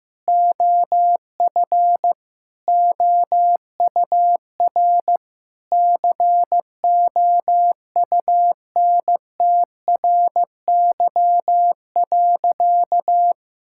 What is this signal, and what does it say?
Morse code, heard as a keyed tone, about 15 wpm, 700 Hz. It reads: OF OUR COUNTRY.
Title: Sentences from Top 1000 Words - Review Part 6 - 15wpm